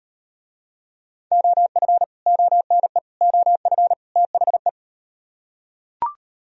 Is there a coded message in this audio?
Yes: Morse code from 1.3 to 4.7 s, heard as a keyed tone, about 35 wpm, 700 Hz. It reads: OF ONE OF THE